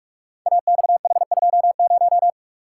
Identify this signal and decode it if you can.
Morse code, heard as a keyed tone, about 44 wpm, 700 Hz. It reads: AXH20
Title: Code Groups - 5 Letter:Numbers - 45wpm